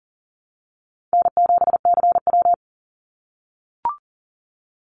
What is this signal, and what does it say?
Morse code, heard as a keyed tone, about 40 wpm, 700 Hz. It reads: N7CW
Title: NA SOTA Activator Callsigns 40wpm